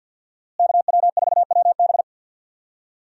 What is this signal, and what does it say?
Morse code, heard as a keyed tone, about 48 wpm, 700 Hz. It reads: KW4WB
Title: USA Call Signs - 5 Characters - Repeated 2x - 50wpm